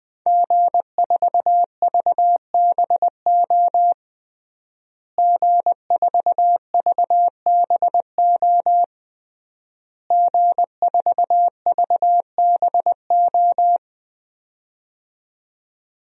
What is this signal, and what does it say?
Morse code, heard as a keyed tone, about 20 wpm, 700 Hz. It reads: G4VBO G4VBO G4VBO
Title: DXCC-100 Call Signs - 5 Characters - Repeated 3x - 20wpm